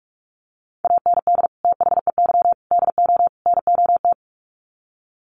Morse code, heard as a keyed tone, about 44 wpm, 700 Hz. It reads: AND THEY DO NOT
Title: Sets of 4 Words 45wpm